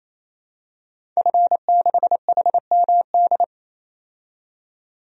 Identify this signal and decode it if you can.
Morse code, heard as a keyed tone, about 28 wpm, 700 Hz. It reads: F6HMD
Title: DXCC-100 Call Signs - 5 Characters - 28wpm